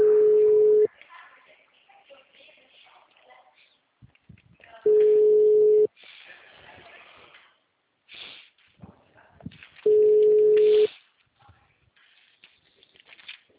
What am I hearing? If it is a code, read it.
Ringback tone (Europe, 425 Hz), with speech.